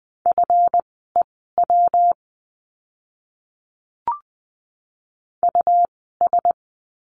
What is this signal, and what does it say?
Morse code, heard as a keyed tone, about 20 wpm, spaced as for 15 wpm, 700 Hz. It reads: FEW US
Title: 230 Lesson 37f - Letter x - Words 20wpm